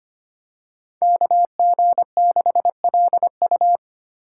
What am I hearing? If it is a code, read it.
Morse code, heard as a keyed tone, about 25 wpm, 700 Hz. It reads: KG6LU
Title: USA Call Signs - 5 Characters - 4x Character Spacing - 25wpm